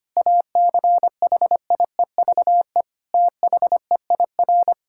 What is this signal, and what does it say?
Morse code, heard as a keyed tone, about 25 wpm, 700 Hz. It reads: ACHIEVE THEIR
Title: Sentences from Top 1000 Words - Review Part 6 - 25wpm